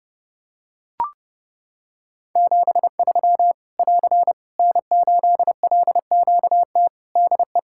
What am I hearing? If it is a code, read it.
Morse code, heard as a keyed tone, about 30 wpm, 700 Hz. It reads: 73 <AR> N8LQT DE
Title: K1USN SST QSO Practice 30wpm